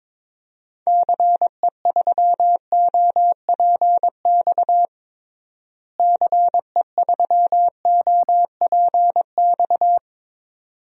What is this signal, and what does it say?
Morse code, heard as a keyed tone, about 22 wpm, 700 Hz. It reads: CE3OPX CE3OPX
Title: DXCC-100 Call Signs - 6 Characters - Repeated 2x - 22wpm